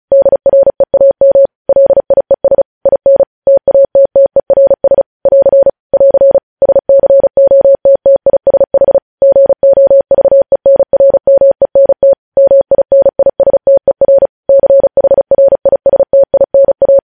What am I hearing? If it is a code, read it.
Morse code, heard as a keyed tone, about 36 wpm, 550 Hz. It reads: DREAM LIES IN TATTERS <AR> <AR> SCOTTISH GOVENRMENT MINISTER CHRISTINA
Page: News headlines in Morse code at 35 wpm